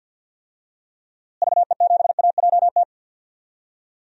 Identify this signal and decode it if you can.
Morse code, heard as a keyed tone, about 51 wpm, 700 Hz. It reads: VE7AJT
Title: DXCC-100 Call Signs - 6 Characters - Repeated 2x - 50wpm